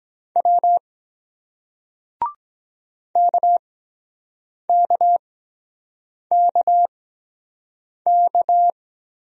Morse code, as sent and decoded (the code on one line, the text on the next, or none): .-- / -.- -.- -.- -.-
W KKKK